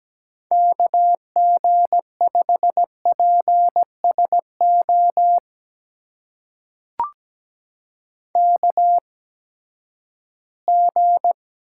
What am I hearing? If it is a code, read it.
Morse code, heard as a keyed tone, about 17 wpm, spaced as for 12 wpm, 700 Hz. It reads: KG5PSO KG
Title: USA Call Signs - 6 Characters - 8x Character Spacing - 17wpm